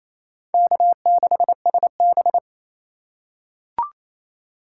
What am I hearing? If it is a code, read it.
Morse code, heard as a keyed tone, about 28 wpm, 700 Hz. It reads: K6SB
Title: USA Call Signs - 4 Characters - 3x Character Spacing - 28wpm